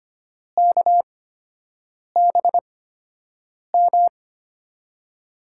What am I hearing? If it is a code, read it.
Morse code, heard as a keyed tone, about 25 wpm, spaced as for 7 wpm, 700 Hz. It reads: KBM
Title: USA Call Signs - 6 Characters - 8x Character Spacing - 25wpm